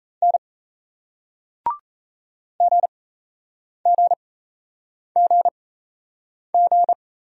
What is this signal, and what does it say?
Morse code, heard as a keyed tone, about 37 wpm, spaced as for 8 wpm, 700 Hz. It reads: N GGGG